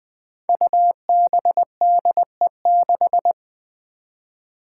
Morse code, heard as a keyed tone, about 20 wpm, 700 Hz. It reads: UBDE6
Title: Five Challenging Letter-Number Combinations 20wpm